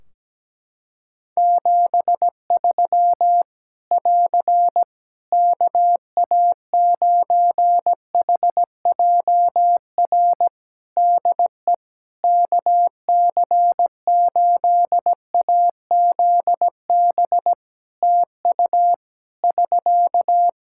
Morse code, heard as a keyed tone, about 17 wpm, 700 Hz. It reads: 73 <AR> KA9HJR DE KC8AZB TU <SK>
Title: Practice QSO - Exchange 4 - Ending - 17wpm